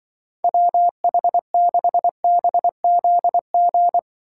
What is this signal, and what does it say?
Morse code, heard as a keyed tone, about 25 wpm, 700 Hz. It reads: WH6BZG
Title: Speed Racing - USA Call Signs 20wpm